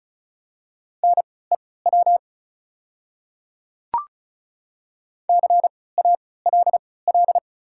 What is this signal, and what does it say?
Morse code, heard as a keyed tone, about 36 wpm, spaced as for 20 wpm, 690 Hz. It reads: NEW CALL
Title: 3x Character Spacing - Top 500 Words - 35wpm